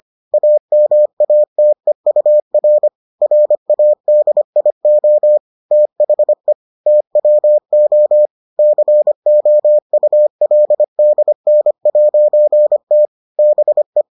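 Morse code, heard as a keyed tone, about 25 wpm, 600 Hz. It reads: AMATEUR RADIO THE TWO COULDN'T BE
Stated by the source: Morse edition of the Foundations of Amateur Radio episodes.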